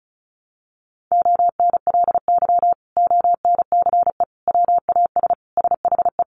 Morse code, heard as a keyed tone, about 34 wpm, 700 Hz. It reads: ONLY ONCE WAS SHE
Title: Sentences from Top 1100 Words 35wpm